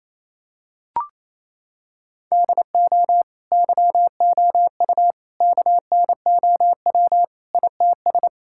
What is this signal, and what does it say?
Morse code, heard as a keyed tone, about 27 wpm, 700 Hz. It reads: DO YOU KNOW ITS
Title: Sentences from Top 1000 Words - Review Part 5 - 28wpm